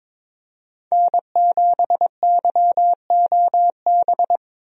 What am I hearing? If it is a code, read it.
Morse code, heard as a keyed tone, about 22 wpm, 700 Hz. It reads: N7YOB
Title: USA Call Signs - 5 Characters - Repeated 2x - 22wpm